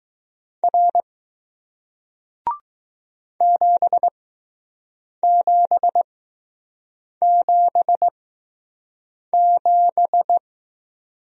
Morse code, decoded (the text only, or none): R 7777